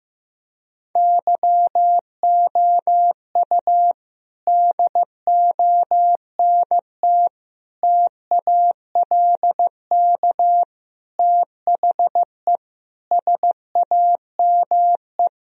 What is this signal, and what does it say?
Morse code, heard as a keyed tone, about 15 wpm, 700 Hz. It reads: YOU DONT TALK THE SAME
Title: Sentences from Top 1000 Words - Review Part 3 - 15wpm